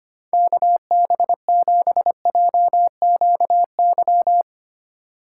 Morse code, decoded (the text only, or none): KB7JQY